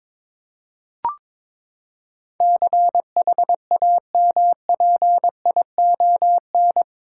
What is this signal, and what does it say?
Morse code, heard as a keyed tone, about 22 wpm, 700 Hz. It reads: CHAMPION
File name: practice / 22wpm / Eight Letter Words - 22wpm